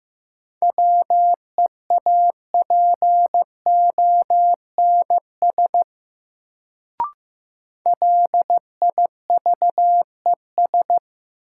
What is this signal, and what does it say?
Morse code, heard as a keyed tone, about 15 wpm, 700 Hz. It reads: WEAPONS LIVES
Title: Sets of 5 Words Encore - New Word Drill - 15wpm